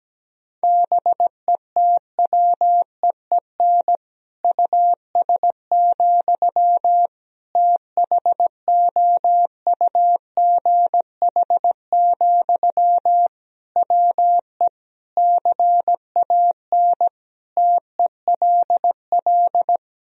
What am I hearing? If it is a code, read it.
Morse code, heard as a keyed tone, about 17 wpm, 700 Hz. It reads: BETWEEN US, THOUGH, WE CAN TELL
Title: Sentences from Top 1000 Words - Review Part 2 - 17wpm